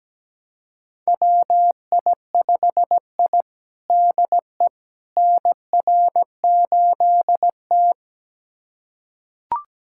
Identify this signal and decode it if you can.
Morse code, heard as a keyed tone, about 17 wpm, 700 Hz. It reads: WI5I DE NR8T
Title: USA Callsign DE Callsign - 4 Characters - 17wpm